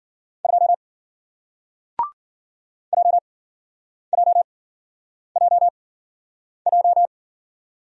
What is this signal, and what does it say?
Morse code, heard as a keyed tone, about 51 wpm, spaced as for 9 wpm, 700 Hz. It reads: 2 JJJJ